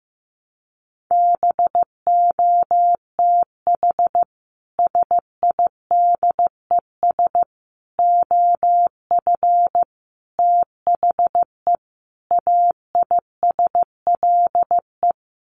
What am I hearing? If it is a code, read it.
Morse code, heard as a keyed tone, about 15 wpm, 700 Hz. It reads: BOTH SIDES OF THE AISLE
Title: Sets of 5 Words Encore2 15wpm